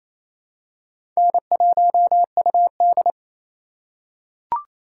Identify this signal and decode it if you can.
Morse code, heard as a keyed tone, about 28 wpm, 700 Hz. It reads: N1UD
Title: USA Call Signs - 4 Characters - 3x Character Spacing - 28wpm